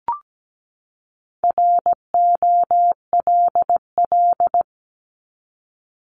Morse code, heard as a keyed tone, about 17 wpm, 700 Hz. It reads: ROLL
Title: Top 1751-2000 Words 17wpm